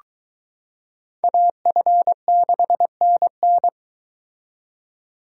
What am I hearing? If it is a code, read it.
Morse code, heard as a keyed tone, about 23 wpm, 700 Hz. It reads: AF6NN